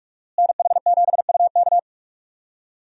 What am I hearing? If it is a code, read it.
Morse code, heard as a keyed tone, about 46 wpm, 700 Hz. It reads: NH7UK
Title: USA Call Signs - 5 Characters - Repeated 2x - 45wpm